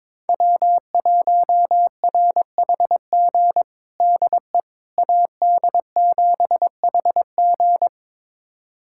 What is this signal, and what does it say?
Morse code, heard as a keyed tone, about 22 wpm, 700 Hz. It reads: W1RHG DE AD7HG
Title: USA Callsign DE Callsign - 5 Characters - 22wpm